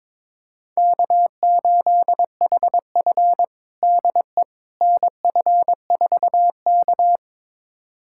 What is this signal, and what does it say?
Morse code, heard as a keyed tone, about 22 wpm, 700 Hz. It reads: K8HF DE NF4K
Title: USA Callsign DE Callsign - 4 Characters - 22wpm